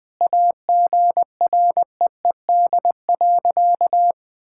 Morse code, heard as a keyed tone, about 21 wpm, 700 Hz. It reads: AGREED.
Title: Sentences with 8-Word Phrases - Spoken Beginning - 20wpm